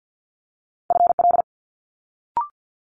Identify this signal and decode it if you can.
Morse code, heard as a keyed tone, about 50 wpm, 700 Hz. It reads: FL
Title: Multi-letter Phonemes Quick Fire 50wpm